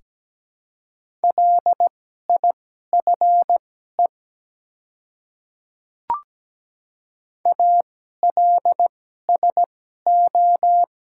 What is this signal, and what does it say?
Morse code, heard as a keyed tone, about 17 wpm, spaced as for 12 wpm, 700 Hz. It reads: LIFE ALSO
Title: 2x Character Spacing - Top 300 Words - 17wpm